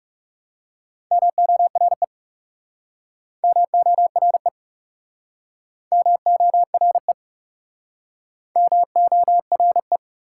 Morse code, heard as a keyed tone, about 42 wpm, 700 Hz. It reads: MORE MORE MORE MORE